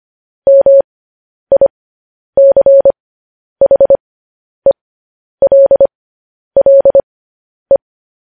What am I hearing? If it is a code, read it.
Morse code, heard as a keyed tone, about 25 wpm, spaced as for 10 wpm, 550 Hz. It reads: MICHELLE
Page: news headlines in morse code at 10 wpm